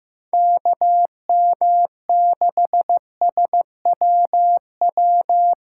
Morse code, heard as a keyed tone, about 15 wpm, 700 Hz. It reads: KM6SWW
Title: USA Callsign DE Callsign 15wpm